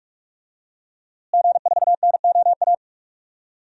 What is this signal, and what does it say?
Morse code, heard as a keyed tone, about 46 wpm, 700 Hz. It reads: G4NOA
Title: DXCC-100 Call Signs - 5 Characters - Repeated 3x - 45wpm